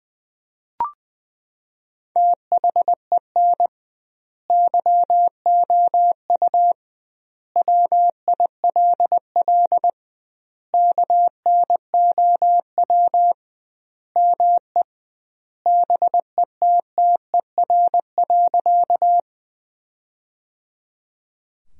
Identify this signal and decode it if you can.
Morse code, heard as a keyed tone, about 20 wpm, 700 Hz. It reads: THEN YOU WILL KNOW ME BETTER.
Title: 2x Word-Spacing - Sentences from Top 400 Words - 20wpm